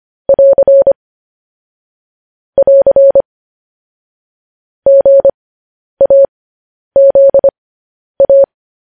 Morse code, heard as a keyed tone, about 25 wpm, spaced as for 10 wpm, 550 Hz. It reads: <AR> <AR> GAZA
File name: podcast / News headlines-1x-25x10